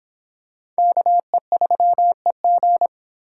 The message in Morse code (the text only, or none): KE3EG